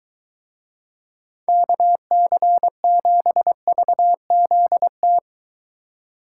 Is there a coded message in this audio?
Yes: Morse code from 1.5 to 5.2 s, heard as a keyed tone, about 23 wpm, 700 Hz. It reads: KC7VZT